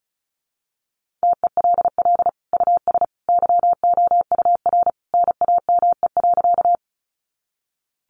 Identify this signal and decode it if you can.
Morse code, heard as a keyed tone, about 34 wpm, 700 Hz. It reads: TELL US YOUR NAME.
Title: Sentences from Top 200 Words 35wpm